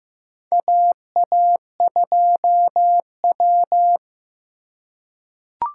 Morse code, heard as a keyed tone, about 15 wpm, 700 Hz. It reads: AA2W
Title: USA Call Signs - 4 Characters - 3x Character Spacing - 15wpm